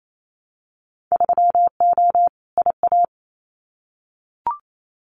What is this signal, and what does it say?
Morse code, heard as a keyed tone, about 28 wpm, 700 Hz. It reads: 3O IA
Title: Winter Field Day Exchanges 28wpm